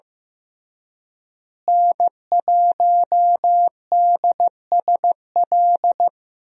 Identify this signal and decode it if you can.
Morse code, heard as a keyed tone, about 15 wpm, 700 Hz. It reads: N1DSL
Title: USA Call Signs - 5 Characters - Repeated 2x - 15wpm